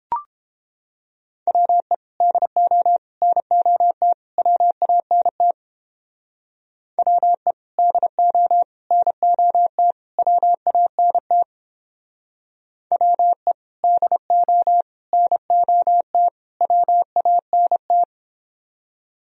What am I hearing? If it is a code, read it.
Morse code, heard as a keyed tone, about 31 wpm, 700 Hz. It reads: WE DO NOT WANT WE DO NOT WANT WE DO NOT WANT